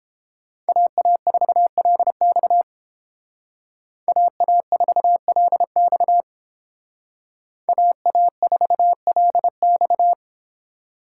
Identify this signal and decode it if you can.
Morse code, heard as a keyed tone, about 30 wpm, 700 Hz. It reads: AA4LX AA4LX AA4LX